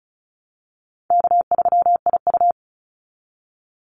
Morse code, heard as a keyed tone, about 34 wpm, 700 Hz. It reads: K3IU
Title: Easy Call Signs Repeated 2x - 35wpm